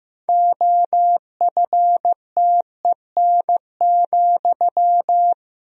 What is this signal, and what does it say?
Morse code, heard as a keyed tone, about 15 wpm, 700 Hz. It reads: OFTEN,
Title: Sentences from Top 1000 Words - Review Part 1 - 15wpm